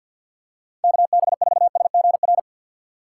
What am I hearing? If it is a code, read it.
Morse code, heard as a keyed tone, about 53 wpm, 700 Hz. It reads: KB4SGR
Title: USA Call Signs - 6 Characters - 3x Character Spacing - 50wpm